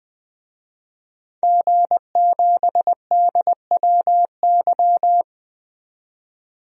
Morse code, heard as a keyed tone, about 20 wpm, 700 Hz. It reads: G7DWY
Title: DXCC-100 Call Signs - 5 Characters - 20wpm